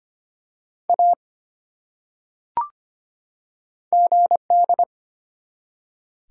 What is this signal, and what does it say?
Morse code, heard as a keyed tone, about 25 wpm, 700 Hz. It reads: A GD
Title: Top 180 QSO Elements 25wpm